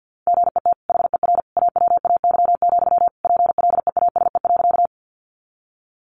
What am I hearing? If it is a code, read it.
Morse code, heard as a keyed tone, about 50 wpm, 700 Hz. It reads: GET HER AWAY, PLEASE.
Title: Sentences from Top 1500 Words 50wpm